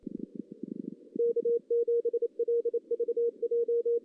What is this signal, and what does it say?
Morse code, heard as a keyed tone, about 32 wpm, 470 Hz. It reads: K7LVJ
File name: 2016 JAN CW NAQP / 2016 Jan naqp k5ot - 0-3